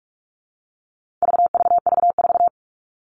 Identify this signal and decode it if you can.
Morse code, heard as a keyed tone, about 44 wpm, 700 Hz. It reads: VVV4
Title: All Cut Numbers - 4-Digit 45wpm